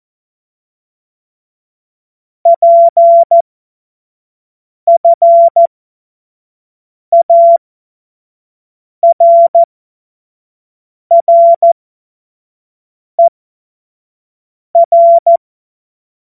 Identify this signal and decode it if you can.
Morse code, heard as a keyed tone, about 14 wpm, spaced as for 5 wpm, 680 Hz. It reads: PFARRER
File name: Morsekurs / Witze / Witz 01. Juni 2021, 25 F70